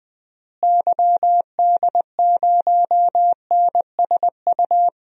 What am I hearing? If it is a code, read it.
Morse code, heard as a keyed tone, about 21 wpm, 700 Hz. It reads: YD0NSU
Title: DXCC-100 Call Signs - 6 Characters - Repeated 3x - 20wpm